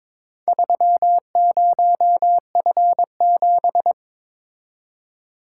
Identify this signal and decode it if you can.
Morse code, heard as a keyed tone, about 22 wpm, 700 Hz. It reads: 30F7
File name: practice / 22wpm / Four Letter-Number Combinations - 22wpm